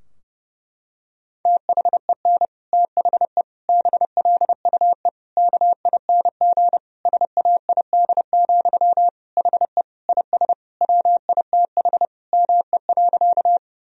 Morse code, heard as a keyed tone, about 32 wpm, 700 Hz. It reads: THEN THE BLUE KING SAID, HE IS WITH ME.
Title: Sentences from Top 1000 Words - Review Part 3 - 30wpm